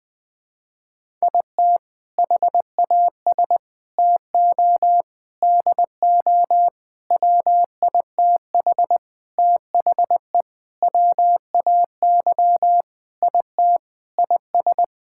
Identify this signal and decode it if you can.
Morse code, heard as a keyed tone, about 20 wpm, 700 Hz. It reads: IT HAS TO DO WITH THE WAY IT IS
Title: Sentences with 5-Word Phrases - Spoken Beginning - 20wpm